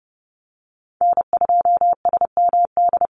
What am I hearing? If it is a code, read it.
Morse code, heard as a keyed tone, about 30 wpm, 700 Hz. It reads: N2SMD
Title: USA Call Signs Repeated 2x 30wpm